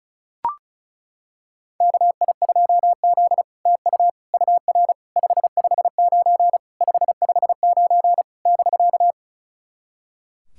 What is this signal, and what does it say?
Morse code, heard as a keyed tone, about 34 wpm, 700 Hz. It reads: KI2Z TU UR 559 559 <BK>
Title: SOTA QSO Practice 35wpm